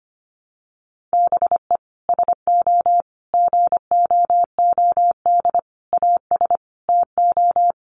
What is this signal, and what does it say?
Morse code, heard as a keyed tone, about 25 wpm, 700 Hz. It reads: BE SO GOOD AS TO
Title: Sentences from Top 900 Words 25wpm